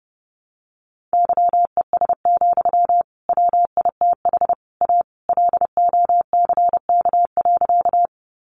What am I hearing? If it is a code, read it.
Morse code, heard as a keyed tone, about 30 wpm, 700 Hz. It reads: YES, WITH A LOCK.
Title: Sentences from Top 1300 Words 30wpm